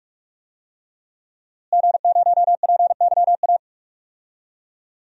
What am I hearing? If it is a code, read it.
Morse code, heard as a keyed tone, about 43 wpm, 700 Hz. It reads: G0PYA
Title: DXCC-100 Call Signs - 5 Characters - Repeated 3x - 45wpm